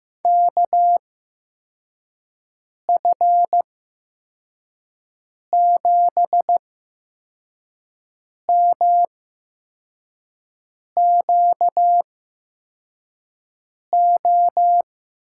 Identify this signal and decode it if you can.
Morse code, heard as a keyed tone, about 15 wpm, spaced as for 4 wpm, 700 Hz. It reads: KF7MQO
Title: USA Call Signs - 6 Characters - 8x Character Spacing - 15wpm